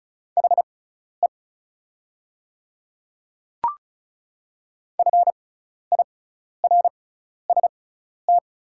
Morse code, heard as a keyed tone, about 36 wpm, spaced as for 12 wpm, 690 Hz. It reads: HE FIRST
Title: 6x Character Spacing - Top 300 Words - 35wpm